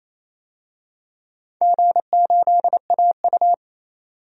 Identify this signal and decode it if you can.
Morse code, heard as a keyed tone, about 28 wpm, 700 Hz. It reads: G8AU
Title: DXCC-100 Call Signs - 4 Characters - 28wpm